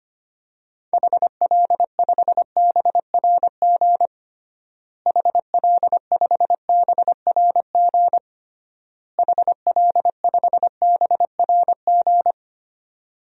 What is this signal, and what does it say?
Morse code, heard as a keyed tone, about 25 wpm, 700 Hz. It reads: HL5BRG HL5BRG HL5BRG
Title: DXCC-100 Call Signs - 6 Characters - Repeated 3x - 25wpm